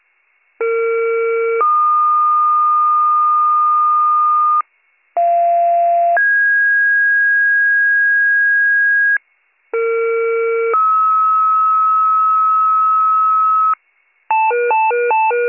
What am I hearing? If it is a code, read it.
Two-tone page, heard as a one-second tone, then a three-second tone, about 460 then 1220 Hz, about 690 then 1650 Hz, about 460 then 1290 Hz. It is heard from 0.6 to 13.7 s.